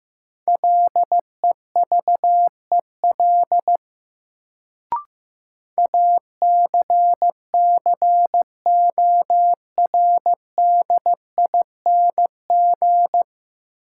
Morse code, heard as a keyed tone, about 15 wpm, 700 Hz. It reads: LEVEL ACCORDING
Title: Sentences from Top 700 Words - New Word Drill - 15wpm